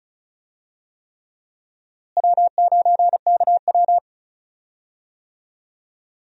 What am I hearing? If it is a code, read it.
Morse code, heard as a keyed tone, about 34 wpm, 700 Hz. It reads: W9KW
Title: USA Call Signs - 4 Characters - Repeated 3x - 35wpm